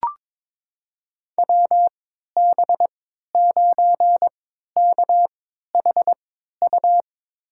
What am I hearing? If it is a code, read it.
Morse code, heard as a keyed tone, about 22 wpm, spaced as for 12 wpm, 700 Hz. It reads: WB9KHU
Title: USA Call Signs - 3x Character Spacing - 22wpm